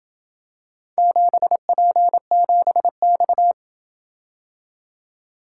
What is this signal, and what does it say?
Morse code, heard as a keyed tone, about 28 wpm, 700 Hz. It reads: 7P7X